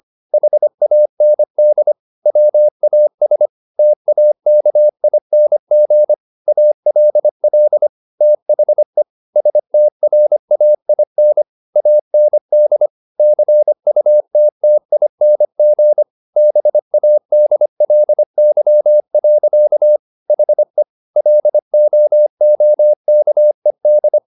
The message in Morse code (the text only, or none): HAND WAS TAKING ALL THE STRAIN AND CUTTING BADLY. HE LOOKED